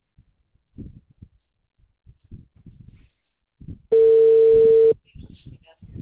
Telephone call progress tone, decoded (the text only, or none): ringback tone